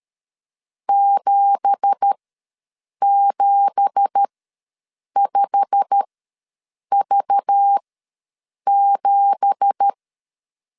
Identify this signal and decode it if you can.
Morse code, heard as a keyed tone, about 13 wpm, spaced as for 7 wpm, 800 Hz. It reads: 775V7
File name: LEZIONI / 9 2867 ws 12wpm 3431